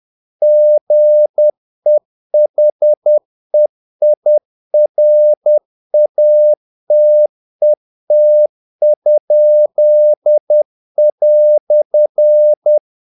Morse code, heard as a keyed tone, about 10 wpm, 600 Hz. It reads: GEHEIRATET?"
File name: dk5ke / Humor-50BpM